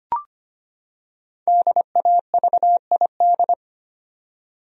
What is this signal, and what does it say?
Morse code, heard as a keyed tone, about 25 wpm, 700 Hz. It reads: DAVID
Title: USA Common Names - Max 5 Letters - 25wpm